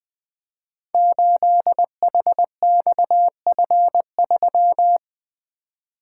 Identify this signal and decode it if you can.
Morse code, heard as a keyed tone, about 20 wpm, 700 Hz. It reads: 8HXF3
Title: Code Groups - 5 Letter:Numbers - 20wpm